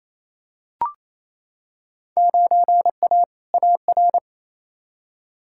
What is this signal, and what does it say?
Morse code, heard as a keyed tone, about 28 wpm, 700 Hz. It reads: 9A AR
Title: Field Day Exchanges Encore - 28wpm